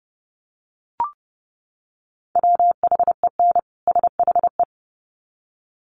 Morse code, heard as a keyed tone, about 30 wpm, 700 Hz. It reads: WHEN SHE
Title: Sets of 2 Words 30wpm